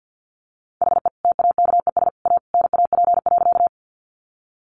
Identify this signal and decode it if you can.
Morse code, heard as a keyed tone, about 50 wpm, 700 Hz. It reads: HE TAKES A NAP.
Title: Easy Phrases 50wpm